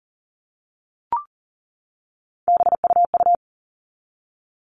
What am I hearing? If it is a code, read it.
Morse code, heard as a keyed tone, about 40 wpm, 700 Hz. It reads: BUU
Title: All Cut Numbers - 3-Digit 40wpm